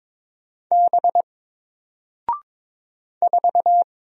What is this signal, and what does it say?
Morse code, heard as a keyed tone, about 22 wpm, 700 Hz. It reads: B 4
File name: Single-Letter-Number-22wpm